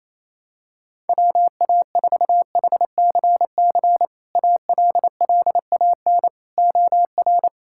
Morse code, heard as a keyed tone, about 28 wpm, 700 Hz. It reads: WA4HCC ALLAN OR
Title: K1USN SST QSO Practice - 28wpm